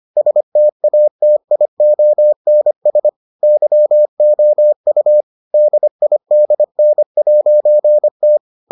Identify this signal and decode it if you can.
Morse code, heard as a keyed tone, about 25 wpm, 600 Hz. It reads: STATIONS YOU DIDN'T